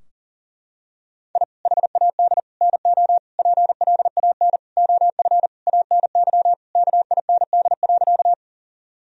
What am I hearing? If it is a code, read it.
Morse code, heard as a keyed tone, about 40 wpm, 700 Hz. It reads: I HAD NO PLAN OF ANY KIND.
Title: Sentences from Top 1000 Words - Review Part 2 - 40wpm